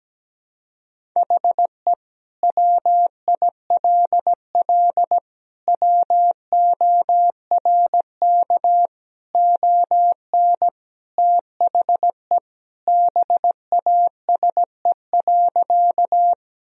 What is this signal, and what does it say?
Morse code, heard as a keyed tone, about 17 wpm, 700 Hz. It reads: HE WILL WORK ON THE BASE.
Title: Sentences from Top 1000 Words - Review Part 1 - 17wpm